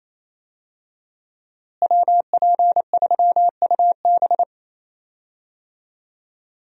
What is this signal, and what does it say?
Morse code, heard as a keyed tone, about 28 wpm, 700 Hz. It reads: WP3UB
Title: USA Call Signs - 5 Characters - Repeated 2x - 28wpm